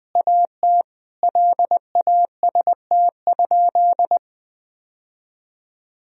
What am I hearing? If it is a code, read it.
Morse code, heard as a keyed tone, about 20 wpm, 700 Hz. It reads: AT LAST?
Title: Sentences from Top 1000 Words - Review Part 4 - 20wpm